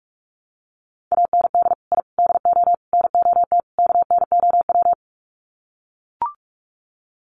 Morse code, heard as a keyed tone, about 44 wpm, 700 Hz. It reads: AND I DO NOT KNOW
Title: Sets of 5 Words 45wpm